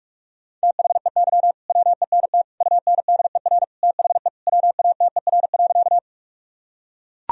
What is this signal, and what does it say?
Morse code, heard as a keyed tone, about 41 wpm, 700 Hz. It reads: THEY WENT UNDER THE WATER.
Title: Sentences from Top 1000 Words - Review Part 1 - 45wpm